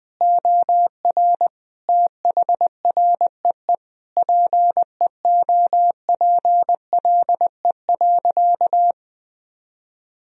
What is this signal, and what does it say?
Morse code, heard as a keyed tone, about 20 wpm, 700 Hz. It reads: OR THREE PEOPLE.
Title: Sentences from Top 1000 Words - Review Part 1 - 20wpm